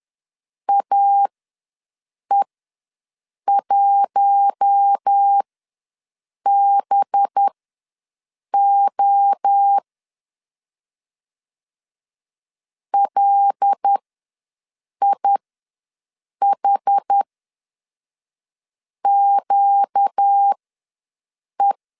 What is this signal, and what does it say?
Morse code, heard as a keyed tone, about 10 wpm, spaced as for 6 wpm, 800 Hz. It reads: AE1BO LIHQE
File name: LEZIONI / 6 PKX0 / 6 PK ws 10wpm 3745